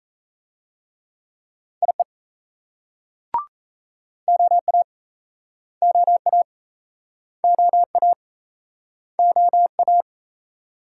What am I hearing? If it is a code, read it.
Morse code, heard as a keyed tone, about 35 wpm, 700 Hz. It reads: IE OA OA OA OA
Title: Speed-Racing Multi-letter Phonemes - 28wpm